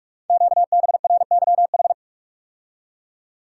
Morse code, heard as a keyed tone, about 44 wpm, 700 Hz. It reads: QBRYH
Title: Code Groups - 5 Letters - 45wpm